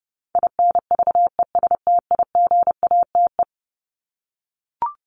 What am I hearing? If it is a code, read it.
Morse code, heard as a keyed tone, about 30 wpm, 700 Hz. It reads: INVESTIGATE
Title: Top 1751-2000 Words 30wpm